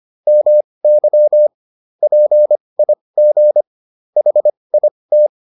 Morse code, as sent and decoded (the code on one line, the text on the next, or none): -- -.-- / .--. .. --. / .... .. -
MY PIG HIT